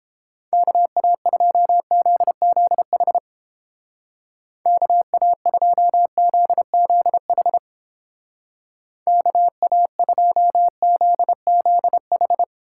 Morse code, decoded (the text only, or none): KA2ZZH KA2ZZH KA2ZZH